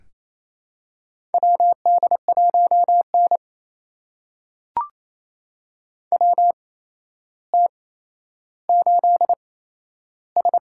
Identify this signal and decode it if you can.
Morse code, heard as a keyed tone, about 28 wpm, spaced as for 12 wpm, 700 Hz. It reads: WD1N WT8S
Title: USA Call Signs - 4 Characters - 8x Character Spacing - 28wpm